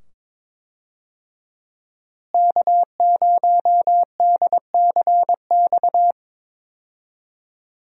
Morse code, heard as a keyed tone, about 22 wpm, 700 Hz. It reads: K0DCX
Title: USA Call Signs - 5 Characters - Repeated 2x - 22wpm